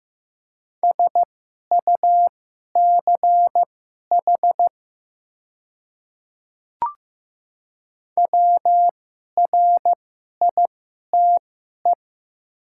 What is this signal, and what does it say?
Morse code, heard as a keyed tone, about 15 wpm, spaced as for 11 wpm, 700 Hz. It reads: SUCH WRITE
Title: 2x Character Spacing - Top 200 Words - 15wpm